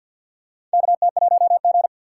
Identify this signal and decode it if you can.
Morse code, heard as a keyed tone, about 51 wpm, 700 Hz. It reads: KT1G
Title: USA Call Signs - 4 Characters - 6x Character Spacing - 50wpm